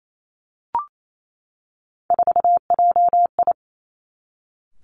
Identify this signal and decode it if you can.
Morse code, heard as a keyed tone, about 28 wpm, 700 Hz. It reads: 4JI
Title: Three Challenging Letter-Number Combinations 28wpm